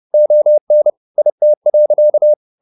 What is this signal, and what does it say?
Morse code, heard as a keyed tone, about 30 wpm, 600 Hz. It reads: ON IT.
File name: the old man and the sea / the old man and the sea 1 30wpm